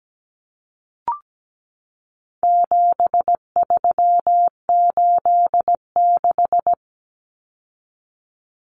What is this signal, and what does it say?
Morse code, heard as a keyed tone, about 17 wpm, 700 Hz. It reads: 7386
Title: Common Cut Numbers - 4-Digit 17wpm